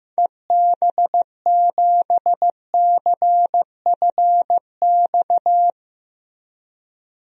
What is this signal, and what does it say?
Morse code, heard as a keyed tone, about 15 wpm, 700 Hz. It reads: EB7CFX
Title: DXCC-100 Call Signs - 6 Characters - 15wpm